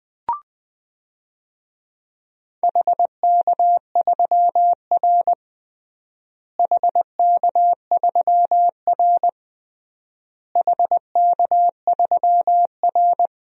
Morse code, heard as a keyed tone, about 20 wpm, 700 Hz. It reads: HK3R HK3R HK3R
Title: DXCC-100 Call Signs - 4 Characters - Repeated 3x - 20wpm